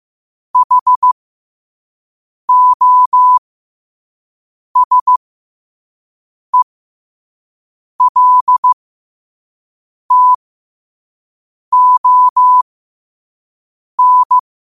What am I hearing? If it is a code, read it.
Morse code, heard as a keyed tone, about 15 wpm, spaced as for 5 wpm, 1000 Hz. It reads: HOSELTON
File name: qotdfiles / 5wpm